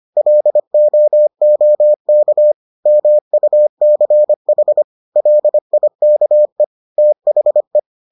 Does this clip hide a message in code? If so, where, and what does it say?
Yes: Morse code from 0.2 to 7.8 s, heard as a keyed tone, about 25 wpm, 600 Hz. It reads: LOOK MUCH LIKE THE